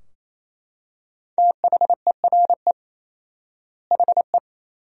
Morse code, heard as a keyed tone, about 28 wpm, 700 Hz. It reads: THERE HE
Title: 4x Word-Spacing - Sentences from Top 100 Words - 28wpm